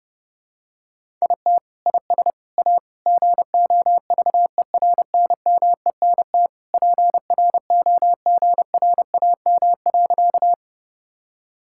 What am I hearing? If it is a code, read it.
Morse code, heard as a keyed tone, about 32 wpm, 700 Hz. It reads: IT IS A GOVERNMENT PROGRAM.
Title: Sentences from Top 1000 Words - Review Part 5 - 30wpm